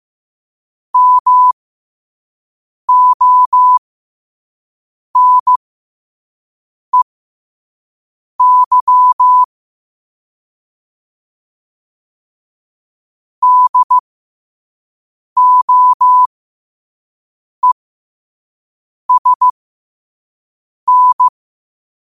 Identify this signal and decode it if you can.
Morse code, heard as a keyed tone, about 15 wpm, spaced as for 5 wpm, 1000 Hz. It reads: MONEY DOESN
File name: qotdfiles / 5wpm